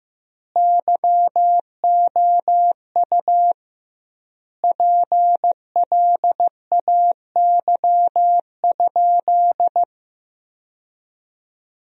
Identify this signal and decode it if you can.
Morse code, heard as a keyed tone, about 15 wpm, 700 Hz. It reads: YOU PLAY?
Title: 2x Word-Spacing - Sentences from Top 300 Words - 15wpm